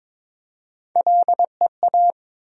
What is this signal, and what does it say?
Morse code, heard as a keyed tone, about 22 wpm, 700 Hz. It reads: LEA
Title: USA Call Sign Postfixes - 3 Letters 22wpm